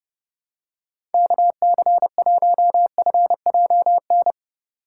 Morse code, heard as a keyed tone, about 30 wpm, 700 Hz. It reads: KC1FJN